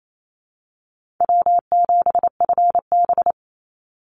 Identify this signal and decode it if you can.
Morse code, heard as a keyed tone, about 28 wpm, 700 Hz. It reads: W7FB
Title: Call Signs - 28wpm